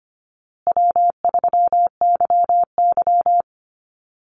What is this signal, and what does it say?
Morse code, heard as a keyed tone, about 25 wpm, 700 Hz. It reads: W3YY
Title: Easy Call Signs Repeated 3x - 25wpm